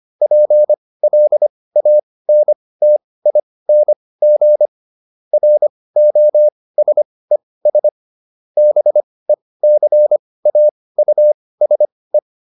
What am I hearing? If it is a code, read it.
Morse code, heard as a keyed tone, about 25 wpm, spaced as for 18 wpm, 600 Hz. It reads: PLANTING ROSES BECAUSE